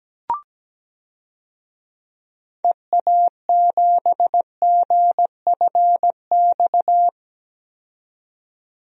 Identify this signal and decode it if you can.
Morse code, heard as a keyed tone, about 17 wpm, 700 Hz. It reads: EA7GFX
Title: DXCC-100 Call Signs - 6 Characters - 17wpm